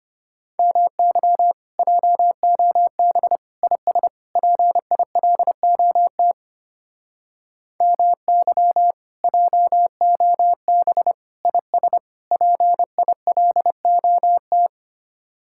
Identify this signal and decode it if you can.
Morse code, heard as a keyed tone, about 30 wpm, 700 Hz. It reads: MY JOB IS PILOT MY JOB IS PILOT
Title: Speed-Racing Common QSO Phrases - 25wpm